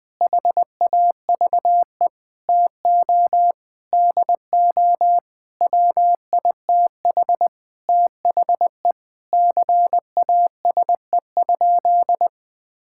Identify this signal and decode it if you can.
Morse code, heard as a keyed tone, about 21 wpm, 700 Hz. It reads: HAVE TO DO WITH THE CASE?
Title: Sentences with 4-Word Phrases - 20wpm